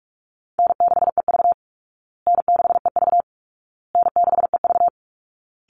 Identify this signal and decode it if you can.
Morse code, heard as a keyed tone, about 44 wpm, 700 Hz. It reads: N6EV N6EV N6EV
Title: Call Signs Repeated 2x - 45wpm